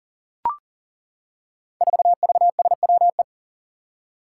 Morse code, heard as a keyed tone, about 40 wpm, 700 Hz. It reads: 4VSWE
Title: Five Challenging Letter-Number Combinations 40wpm